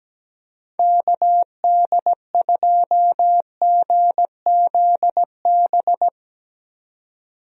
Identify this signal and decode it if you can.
Morse code, heard as a keyed tone, about 17 wpm, 700 Hz. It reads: KD2GZB
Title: USA Call Signs - 6 Characters - 17wpm